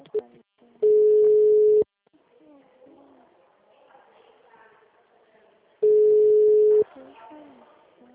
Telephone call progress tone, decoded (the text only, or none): ringback tone